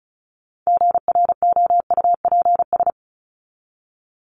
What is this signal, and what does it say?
Morse code, heard as a keyed tone, about 34 wpm, 700 Hz. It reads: GROUPS
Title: ES:S Suffix - Top 100 Words with Base Words 35wpm